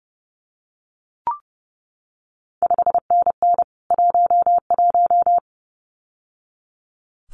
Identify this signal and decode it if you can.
Morse code, heard as a keyed tone, about 30 wpm, 700 Hz. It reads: 5NN 11
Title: CQ WW DX Contest Exchange Encore - 30wpm